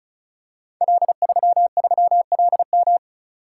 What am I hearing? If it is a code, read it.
Morse code, heard as a keyed tone, about 36 wpm, 690 Hz. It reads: L33LM
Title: Code Groups - 5 Letter:Numbers - 35wpm